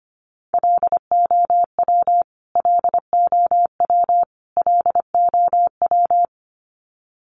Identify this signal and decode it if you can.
Morse code, heard as a keyed tone, about 25 wpm, 700 Hz. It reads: LOW LOW LOW
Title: Top 200 Words Repeated 25wpm